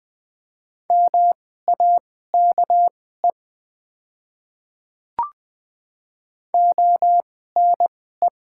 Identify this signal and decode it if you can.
Morse code, heard as a keyed tone, about 20 wpm, spaced as for 15 wpm, 700 Hz. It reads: MAKE ONE
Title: 2x Character Spacing - Top 100 Words - 20wpm